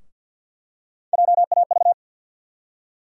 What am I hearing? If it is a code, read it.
Morse code, heard as a keyed tone, about 53 wpm, 700 Hz. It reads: JAV
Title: USA Call Sign Postfixes - 3 Letters 50wpm